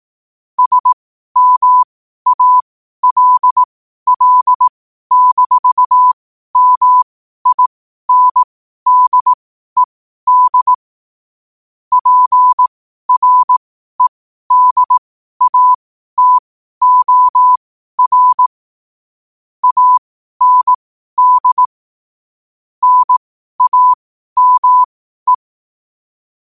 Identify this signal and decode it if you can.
Morse code, heard as a keyed tone, about 18 wpm, spaced as for 12 wpm, 1000 Hz. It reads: SMALL-MINDED PREDATOR AND NAME
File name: 12wpm